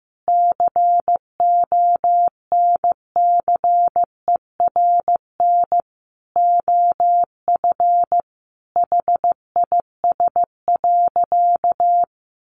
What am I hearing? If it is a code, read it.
Morse code, heard as a keyed tone, about 14 wpm, 700 Hz. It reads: CONCERN OF HIS.
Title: Sentences from Top 1000 Words 15wpm